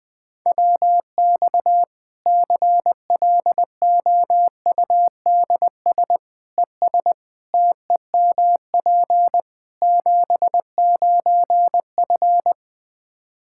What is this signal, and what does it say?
Morse code, heard as a keyed tone, about 21 wpm, 700 Hz. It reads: WX CLOUDS ES TEMP 79F
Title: Practice QSO - Exchange 2 - Info - 20wpm